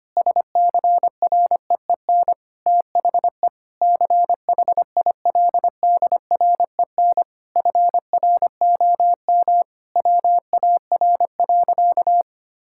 Morse code, heard as a keyed tone, about 25 wpm, 700 Hz. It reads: SCREEN THE CHILDREN FROM WAR.
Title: Sentences from Top 1000 Words - Review Part 6 - 25wpm